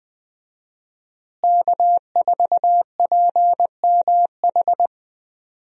Morse code, heard as a keyed tone, about 20 wpm, 700 Hz. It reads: K4PMH
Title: USA Call Signs - 5 Characters - Repeated 3x - 20wpm